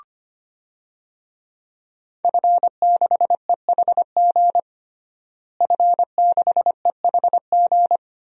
Morse code, heard as a keyed tone, about 25 wpm, 700 Hz. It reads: F6EHG F6EHG
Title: DXCC-100 Call Signs - 5 Characters - Repeated 3x - 25wpm